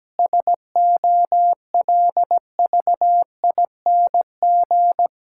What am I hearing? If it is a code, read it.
Morse code, heard as a keyed tone, about 17 wpm, 700 Hz. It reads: SOLVING
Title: Pangram Challenge v2 - 17wpm